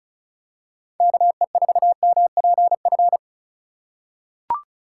Morse code, heard as a keyed tone, about 36 wpm, 700 Hz. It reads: KE4MPF
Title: USA Call Signs - 6 Characters - 3x Character Spacing - 35wpm